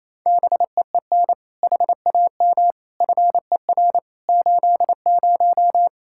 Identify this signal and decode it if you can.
Morse code, heard as a keyed tone, about 27 wpm, 700 Hz. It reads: BEEN HAM FER 80
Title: K1USN SST QSO Practice 28wpm